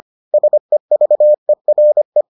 Morse code, heard as a keyed tone, about 25 wpm, 600 Hz. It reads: SEVERE